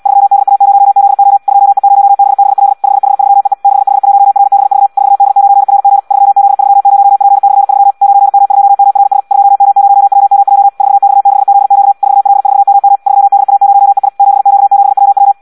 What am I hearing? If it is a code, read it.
Morse code, heard as a keyed tone, about 74 wpm, 800 Hz. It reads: 3DA0ZO 2E0BBL 5H3EE 6H1RBV 4D9RG 3Z30P/3 9M2MRS 2M0WMJ 4X6ZQ 4L4MM 3DA0EI 7Q7WW
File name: Audio practica CW / qrq-tx-74 2